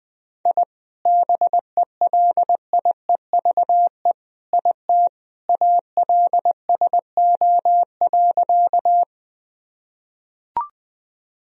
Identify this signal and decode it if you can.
Morse code, heard as a keyed tone, about 20 wpm, 700 Hz. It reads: I BELIEVE IT ALSO.
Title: Sentences from Top 1000 Words - Review Part 1 - 20wpm